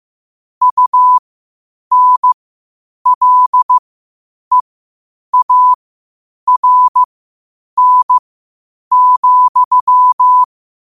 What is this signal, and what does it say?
Morse code, heard as a keyed tone, about 15 wpm, spaced as for 8 wpm, 1000 Hz. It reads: UNLEARN,
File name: qotdfiles / 8wpm